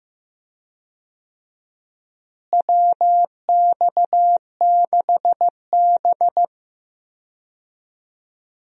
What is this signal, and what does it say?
Morse code, heard as a keyed tone, about 15 wpm, 700 Hz. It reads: WX6B
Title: USA Call Signs - 4 Characters - Repeated 2x - 15wpm